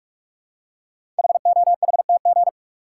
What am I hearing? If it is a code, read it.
Morse code, heard as a keyed tone, about 44 wpm, 690 Hz. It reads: HOHTG
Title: Code Groups - 5 Letters - 45wpm